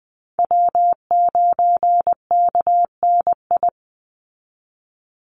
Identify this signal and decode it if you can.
Morse code, heard as a keyed tone, about 20 wpm, 700 Hz. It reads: W9KNI
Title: Call Signs Repeated 3x - 20wpm